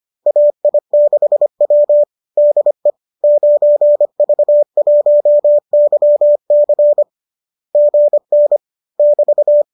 Morse code, heard as a keyed tone, about 25 wpm, 590 Hz. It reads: AI6W DE 9V1YC GN <BT>